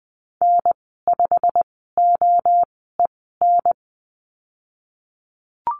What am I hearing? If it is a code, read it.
Morse code, heard as a keyed tone, about 20 wpm, spaced as for 15 wpm, 700 Hz. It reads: N5OEN
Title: 088 Lesson 15g - Number 5 - Call Signs 20wpm